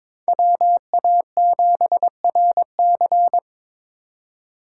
Morse code, heard as a keyed tone, about 22 wpm, 700 Hz. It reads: WA7RC
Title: USA Call Signs - 5 Characters - 3x Character Spacing - 22wpm